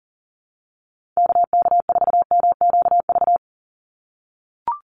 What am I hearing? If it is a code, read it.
Morse code, heard as a keyed tone, about 40 wpm, 700 Hz. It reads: KK4MQV
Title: USA Call Signs 40wpm